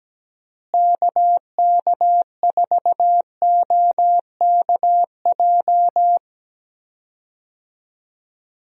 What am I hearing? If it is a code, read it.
Morse code, heard as a keyed tone, about 17 wpm, 700 Hz. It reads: KK4OKJ
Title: USA Call Signs - 6 Characters - Repeated 2x - 17wpm